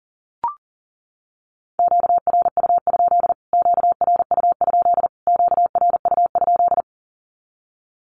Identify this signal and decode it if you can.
Morse code, heard as a keyed tone, about 37 wpm, 700 Hz. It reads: QRU? QRU? QRU?
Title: Q-Codes Repeated 3x 40wpm